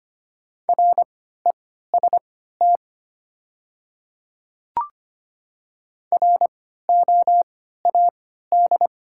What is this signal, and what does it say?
Morse code, heard as a keyed tone, about 25 wpm, spaced as for 14 wpm, 690 Hz. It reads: REST ROAD
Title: 3x Character Spacing - Top 500 Words - 25wpm